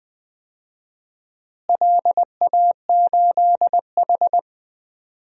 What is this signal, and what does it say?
Morse code, heard as a keyed tone, about 20 wpm, 700 Hz. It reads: LA8H
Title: DXCC-100 Call Signs - 4 Characters - Repeated 3x - 20wpm